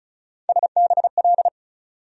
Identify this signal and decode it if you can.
Morse code, heard as a keyed tone, about 36 wpm, 700 Hz. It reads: SBL
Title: USA Call Sign Postfixes - 3 Letters 35wpm